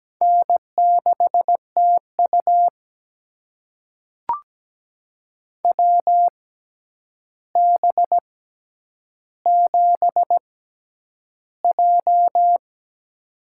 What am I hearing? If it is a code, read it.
Morse code, heard as a keyed tone, about 17 wpm, spaced as for 9 wpm, 700 Hz. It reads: N6TU WB7J
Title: USA Call Signs - 4 Characters - 6x Character Spacing - 17wpm